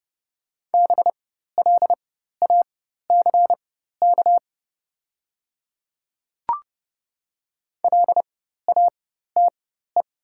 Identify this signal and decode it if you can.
Morse code, heard as a keyed tone, about 30 wpm, spaced as for 14 wpm, 700 Hz. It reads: BLACK LATE
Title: 4x Character Spacing - Top 500 Words - 30wpm